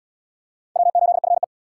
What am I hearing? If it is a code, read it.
Morse code, heard as a keyed tone, about 70 wpm, 700 Hz. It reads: W8FE